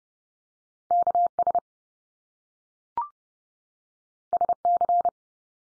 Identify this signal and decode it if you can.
Morse code, heard as a keyed tone, about 30 wpm, 700 Hz. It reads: KS SC
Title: US States 30WPM